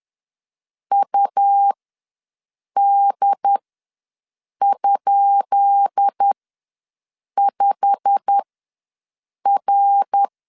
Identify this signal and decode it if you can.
Morse code, heard as a keyed tone, about 10 wpm, spaced as for 6 wpm, 800 Hz. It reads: UD?5R
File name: LEZIONI / 10 SIGNS / 10B INT ws 10wpm 4134